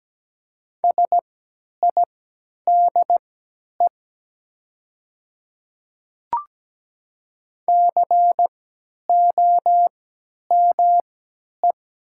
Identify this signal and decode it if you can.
Morse code, heard as a keyed tone, about 17 wpm, spaced as for 10 wpm, 700 Hz. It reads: SIDE COME
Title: 3x Character Spacing - Top 100 Words - 17wpm